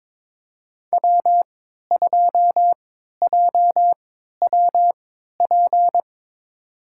Morse code, heard as a keyed tone, about 22 wpm, spaced as for 13 wpm, 700 Hz. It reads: W2JWP
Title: USA Call Signs - 5 Characters - 3x Character Spacing - 22wpm